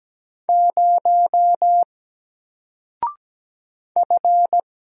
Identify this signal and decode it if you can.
Morse code, heard as a keyed tone, about 17 wpm, 700 Hz. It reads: 0 F
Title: Single Character (Letter:Number:Punctuation:Prosign) Alt 17wpm